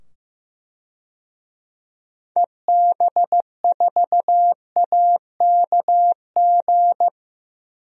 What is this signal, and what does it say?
Morse code, heard as a keyed tone, about 15 wpm, 700 Hz. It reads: EB4AKG
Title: DXCC-100 Call Signs - 6 Characters - 15wpm